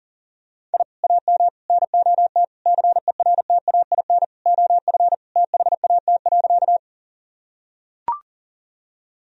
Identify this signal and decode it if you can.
Morse code, heard as a keyed tone, about 40 wpm, 700 Hz. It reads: I AM NOT CERTAIN OF THAT.
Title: Sentences from Top 1000 Words - Review Part 2 - 40wpm